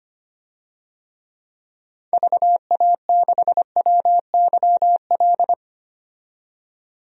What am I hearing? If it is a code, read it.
Morse code, heard as a keyed tone, about 25 wpm, 700 Hz. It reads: VA6WYL
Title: DXCC-100 Call Signs - 6 Characters - 25wpm